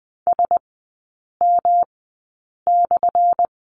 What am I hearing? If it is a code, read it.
Morse code, heard as a keyed tone, about 20 wpm, spaced as for 8 wpm, 690 Hz. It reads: SM/
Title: 212 Lesson 34e - Number 8 - Three Characters 20wpm